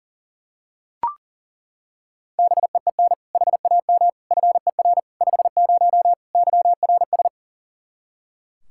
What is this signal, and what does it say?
Morse code, heard as a keyed tone, about 40 wpm, 700 Hz. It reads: BEEN HAM FER 50 YRS
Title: Practice QSO - Exchange 3 - More Info - 40wpm